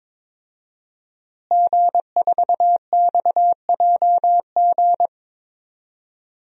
Morse code, heard as a keyed tone, about 22 wpm, 700 Hz. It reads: G4XJG
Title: DXCC-100 Call Signs - 5 Characters - Repeated 3x - 22wpm